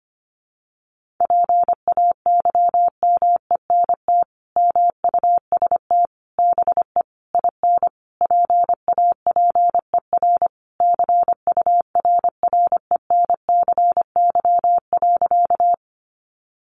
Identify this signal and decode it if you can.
Morse code, heard as a keyed tone, about 25 wpm, 700 Hz. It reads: PAYMENT MUST BE IN PAPER CURRENCY.
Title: Sentences from Top 1500 Words 25wpm